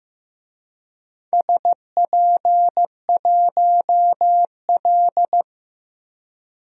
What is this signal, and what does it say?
Morse code, heard as a keyed tone, about 15 wpm, 700 Hz. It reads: SP1L
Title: DXCC-100 Call Signs - 4 Characters - 15wpm